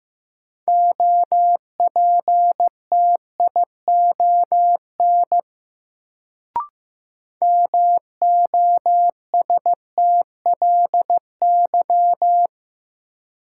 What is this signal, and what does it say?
Morse code, heard as a keyed tone, about 15 wpm, 700 Hz. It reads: OPTION MOSTLY
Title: Sentences from Top 1300 Words - New Word Drill Alt - 15wpm